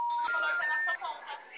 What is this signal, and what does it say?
Special information tone, with speech.
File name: out- -2011- - - .3630